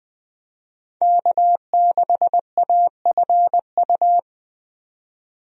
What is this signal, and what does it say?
Morse code, heard as a keyed tone, about 21 wpm, 700 Hz. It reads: K6AFU
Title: USA Call Signs - 5 Characters - 20wpm